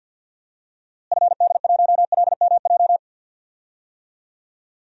Morse code, heard as a keyed tone, about 49 wpm, 700 Hz. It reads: FD1LMJ
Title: DXCC-100 Call Signs - 6 Characters - Repeated 2x - 50wpm